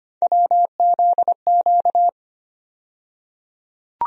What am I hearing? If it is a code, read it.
Morse code, heard as a keyed tone, about 25 wpm, 700 Hz. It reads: WZQ
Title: USA Call Sign Postfixes - 3 Letters 25wpm